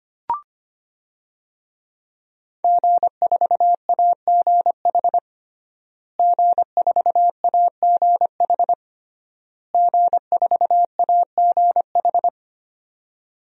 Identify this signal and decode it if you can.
Morse code, heard as a keyed tone, about 25 wpm, 700 Hz. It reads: G4AGH G4AGH G4AGH
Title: DXCC-100 Call Signs - 5 Characters - Repeated 3x - 25wpm